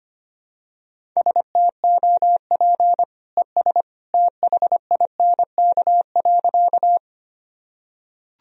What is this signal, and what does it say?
Morse code, heard as a keyed tone, about 25 wpm, 700 Hz. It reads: STOP ES THINK.
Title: Sentences from Top 300 Words and Abbreviations 25wpm